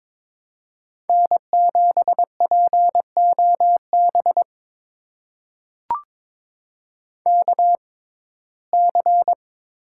Morse code, heard as a keyed tone, about 21 wpm, spaced as for 16 wpm, 700 Hz. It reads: N7POB KC
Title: USA Call Signs - 5 Characters - 6x Character Spacing - 22wpm